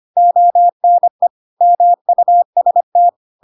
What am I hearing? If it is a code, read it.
Morse code, heard as a keyed tone, about 25 wpm, 700 Hz. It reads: ONE MUST
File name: Pinocchio / Pinocchio ch7 25 25 ik1hge